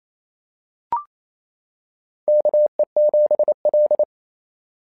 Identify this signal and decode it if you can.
Morse code, heard as a keyed tone, about 27 wpm, 660 Hz. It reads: KE7L
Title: SOTA QSO Practice 28wpm